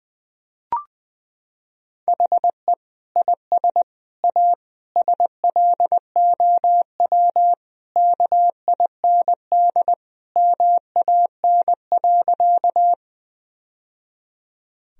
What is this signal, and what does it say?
Morse code, heard as a keyed tone, about 20 wpm, 700 Hz. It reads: HE IS A SLOW KIND MAN.
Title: Sentences from Top 1000 Words - Review Part 2 - 20wpm